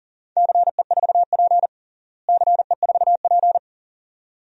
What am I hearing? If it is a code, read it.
Morse code, heard as a keyed tone, about 40 wpm, 700 Hz. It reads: CE4P CE4P
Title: DXCC-100 Call Signs - 4 Characters - Repeated 2x - 40wpm